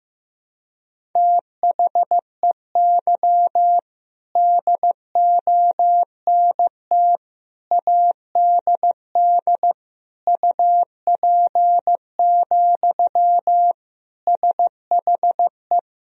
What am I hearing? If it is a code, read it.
Morse code, heard as a keyed tone, about 16 wpm, 700 Hz. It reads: THEY DONT ADD UP, SHE
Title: Sentences from Top 1000 Words - Review Part 1 - 15wpm